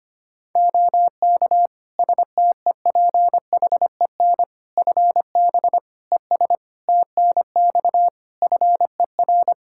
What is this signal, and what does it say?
Morse code, heard as a keyed tone, about 25 wpm, 700 Hz. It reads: OK STEPHEN FB ES TNX FER
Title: Practice QSO - Exchange 2 - Info - 25wpm